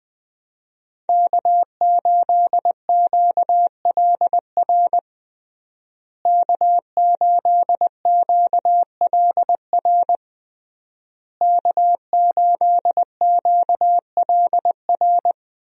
Morse code, heard as a keyed tone, about 19 wpm, 700 Hz. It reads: K8QLR K8QLR K8QLR
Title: USA Call Signs - 5 Characters - Repeated 3x - 20wpm